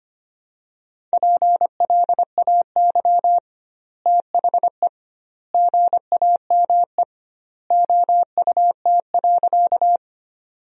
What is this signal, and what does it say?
Morse code, heard as a keyed tone, about 25 wpm, 700 Hz. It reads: PLAY THE GAME OUT.
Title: 2x Word-Spacing - Sentences from Top 500 Words - 25wpm